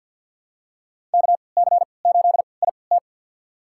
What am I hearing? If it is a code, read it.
Morse code, heard as a keyed tone, about 51 wpm, spaced as for 28 wpm, 700 Hz. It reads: KC8IT
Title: USA Call Signs - 3x Character Spacing - 50wpm